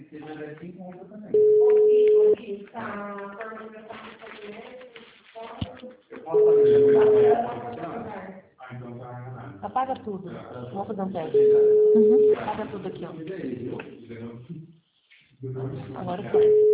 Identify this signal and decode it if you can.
Ringback tone (Europe, 425 Hz), with speech.
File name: out- -2029- - - .2975